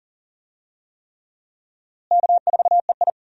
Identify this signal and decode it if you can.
Morse code, heard as a keyed tone, about 40 wpm, 700 Hz. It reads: K4EI
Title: USA Call Signs - 4 Characters - Repeated 3x - 40wpm